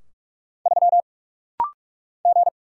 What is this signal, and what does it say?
Morse code, heard as a keyed tone, about 46 wpm, 700 Hz. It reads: 3 G
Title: Single Letter-Number - Rapid-Fire 45wpm